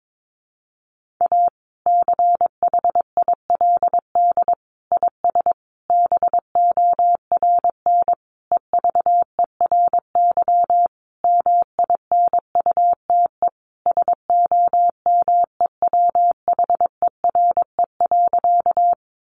Morse code, heard as a keyed tone, about 22 wpm, 700 Hz. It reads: A CHILD IS BORN EVERY MINUTE SOMEWHERE.
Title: Sentences from Top 1400 Words 22wpm